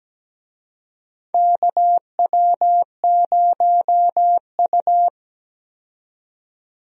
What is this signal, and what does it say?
Morse code, heard as a keyed tone, about 17 wpm, 700 Hz. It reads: KW0U
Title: USA Call Signs - 4 Characters - 17wpm